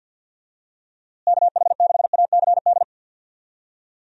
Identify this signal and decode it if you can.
Morse code, heard as a keyed tone, about 50 wpm, 700 Hz. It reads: KH6ACD
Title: USA Call Signs - 6 Characters - Repeated 3x - 50wpm